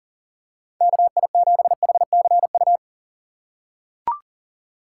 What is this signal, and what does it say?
Morse code, heard as a keyed tone, about 38 wpm, 700 Hz. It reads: KI7HCU
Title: USA Call Signs - 6 Characters - 4x Character Spacing - 40wpm